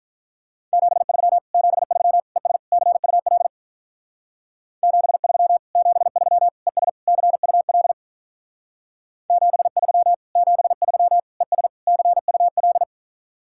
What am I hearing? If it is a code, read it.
Morse code, heard as a keyed tone, about 44 wpm, 700 Hz. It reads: 73 73 ES CUL 73 73 ES CUL 73 73 ES CUL